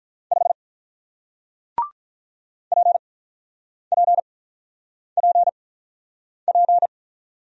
Morse code, heard as a keyed tone, about 48 wpm, spaced as for 9 wpm, 690 Hz. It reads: 5 PPPP